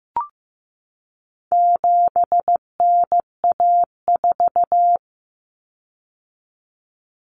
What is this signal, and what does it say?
Morse code, heard as a keyed tone, about 15 wpm, 700 Hz. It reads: 7NA4
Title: Common Cut Numbers - 4-Digit 15wpm